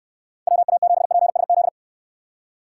Morse code, heard as a keyed tone, about 66 wpm, 700 Hz. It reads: WA7PSZ